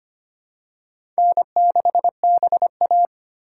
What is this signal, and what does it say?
Morse code, heard as a keyed tone, about 25 wpm, 700 Hz. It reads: N6BA
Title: USA Call Signs - 4 Characters - 2x Character Spacing - 25wpm